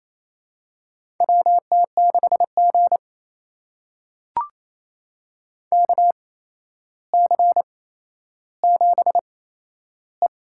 Morse code, heard as a keyed tone, about 28 wpm, spaced as for 12 wpm, 700 Hz. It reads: WT6G KC7E
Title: USA Call Signs - 4 Characters - 8x Character Spacing - 28wpm